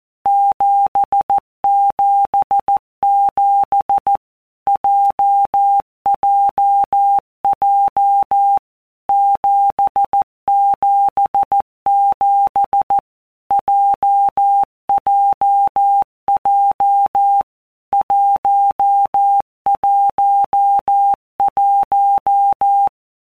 Morse code, heard as a keyed tone, about 14 wpm, 800 Hz. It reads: 777 JJJ 777 JJJ 111
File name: Basic Letter Groups By WPM's / Groups@13 / JZ17@13